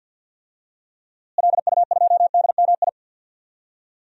Morse code, heard as a keyed tone, about 47 wpm, 700 Hz. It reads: LU2DMI
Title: DXCC-100 Call Signs - 6 Characters - Repeated 3x - 50wpm